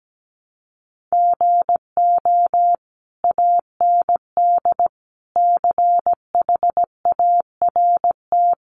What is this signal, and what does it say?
Morse code, heard as a keyed tone, about 17 wpm, 700 Hz. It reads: GO AND CHART
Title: Sentences from Top 1300 Words 17wpm